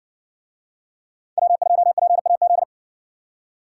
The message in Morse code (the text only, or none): W2PAZ